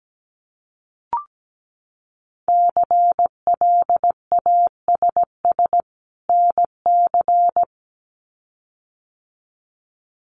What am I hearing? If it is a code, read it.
Morse code, heard as a keyed tone, about 17 wpm, 700 Hz. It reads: CLASS NC
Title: Winter Field Day Exchanges 17wpm